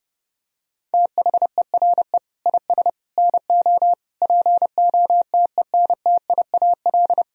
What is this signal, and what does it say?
Morse code, heard as a keyed tone, about 32 wpm, 700 Hz. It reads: THERE IS NO POTENTIAL
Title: Sentences from Top 1000 Words - Review Part 4 - 30wpm